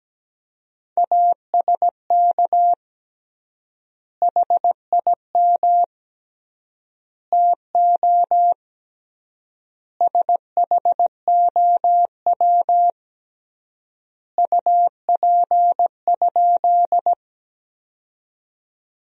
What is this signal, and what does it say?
Morse code, heard as a keyed tone, about 17 wpm, 700 Hz. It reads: ASK HIM TO SHOW UP?
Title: 3x Word-Spacing - Sentences from Top 200 Words - 17wpm